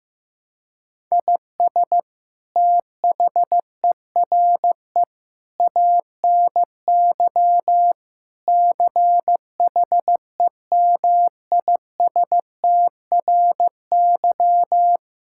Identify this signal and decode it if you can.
Morse code, heard as a keyed tone, about 15 wpm, 700 Hz. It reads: IS THERE ANY CHEMISTRY
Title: Sentences from Top 1000 Words - Review Part 6 - 15wpm